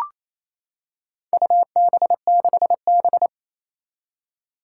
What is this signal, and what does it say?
Morse code, heard as a keyed tone, about 27 wpm, 700 Hz. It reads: UB6B
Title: Four Challenging Letter-Number Combinations 28wpm